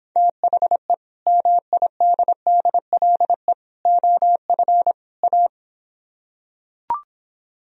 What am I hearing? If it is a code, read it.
Morse code, heard as a keyed tone, about 26 wpm, 700 Hz. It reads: THE MIDDLE OF A